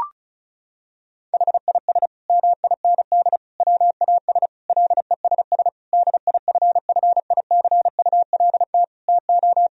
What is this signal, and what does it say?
Morse code, heard as a keyed tone, about 34 wpm, 700 Hz. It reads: HIS MIND WAS LESS DIFFICULT TO
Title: Sentences from Top 1000 Words - Review Part 2 - 35wpm